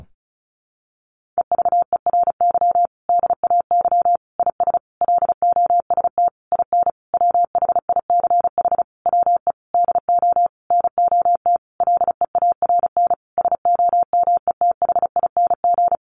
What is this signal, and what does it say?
Morse code, heard as a keyed tone, about 34 wpm, 700 Hz. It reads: EVERY DAY IS LOST IN WHICH WE DO NOT LEARN SOMETHING
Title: Sentences from Top 1400 Words 35wpm